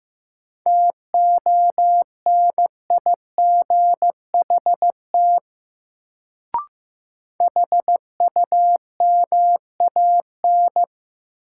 Morse code, heard as a keyed tone, about 15 wpm, 700 Hz. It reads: TONIGHT HUMAN
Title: Sentences from Top 600 Words - New Word Drill - 15wpm